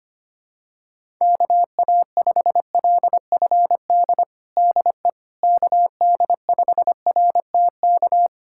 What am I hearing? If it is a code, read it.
Morse code, heard as a keyed tone, about 25 wpm, 700 Hz. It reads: KA5LFD DE KD5RTK
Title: USA Callsign DE Callsign - 6 Characters - 25wpm